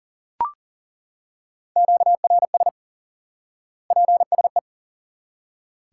Morse code, heard as a keyed tone, about 40 wpm, 700 Hz. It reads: QRS PSE
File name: IQE Course 3 - Lesson 89 - YAGI - Two Elements - 40wpm